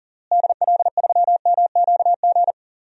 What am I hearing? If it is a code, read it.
Morse code, heard as a keyed tone, about 40 wpm, 700 Hz. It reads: DL3MQG
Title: DXCC-100 Call Signs - 6 Characters - Repeated 3x - 40wpm